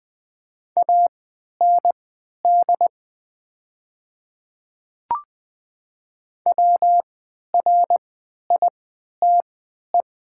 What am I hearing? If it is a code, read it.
Morse code, heard as a keyed tone, about 20 wpm, spaced as for 11 wpm, 700 Hz. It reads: AND WRITE
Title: 3x Character Spacing - Top 100 Words - 20wpm